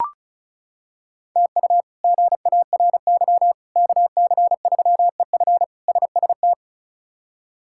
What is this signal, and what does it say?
Morse code, heard as a keyed tone, about 36 wpm, 700 Hz. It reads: TU GARY KC3EF SST
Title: K1USN SST QSO Practice - 35wpm